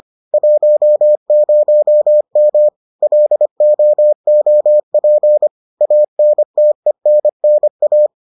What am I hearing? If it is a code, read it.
Morse code, heard as a keyed tone, about 25 wpm, 600 Hz. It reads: 10M LOOP ANTENNA